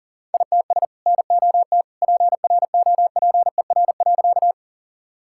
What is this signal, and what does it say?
Morse code, heard as a keyed tone, about 40 wpm, 700 Hz. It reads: ITS NOT PROPER.
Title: Sentences from Top 1000 Words - Review Part 5 - 40wpm